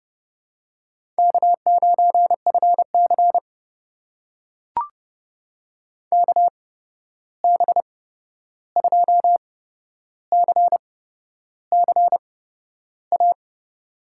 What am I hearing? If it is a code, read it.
Morse code, heard as a keyed tone, about 30 wpm, spaced as for 11 wpm, 700 Hz. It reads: K9FC KB2CCA
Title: USA Call Signs - 8x Character Spacing - 30wpm